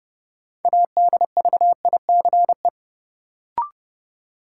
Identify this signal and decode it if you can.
Morse code, heard as a keyed tone, about 30 wpm, 700 Hz. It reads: ADVICE
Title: Sentences from Top 1100 Words - New Word Drill - 30wpm